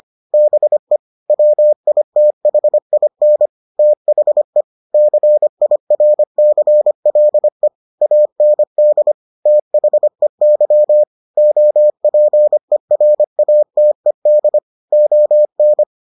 Morse code, heard as a keyed tone, about 25 wpm, 600 Hz. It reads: BE WITHIN THE CIRCLE AND THEY OPERATED ON